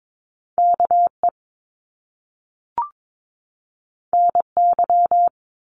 Morse code, heard as a keyed tone, about 22 wpm, 700 Hz. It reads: KE NY
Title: Top Two Letter Combinations in Words 22wpm